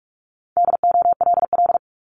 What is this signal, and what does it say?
Morse code, heard as a keyed tone, about 46 wpm, 700 Hz. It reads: DOLL
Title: Multi-letter Phonemes 45wpm